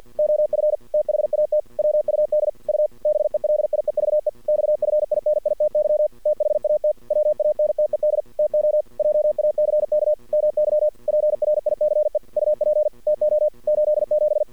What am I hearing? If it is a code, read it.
Morse code, heard as a keyed tone, about 52 wpm, 610 Hz. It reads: OF THAT WUD U BELIEVE XFINITY THAT WANTED TO JACK MY PRICE UP TO 87
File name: archived / CRSnet